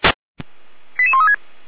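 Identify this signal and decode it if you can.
Five-tone selective call (ZVEI), heard as quick runs of five tones, tones of about 70 ms. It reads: 88136
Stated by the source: ZVEI five-tone call received by SDR.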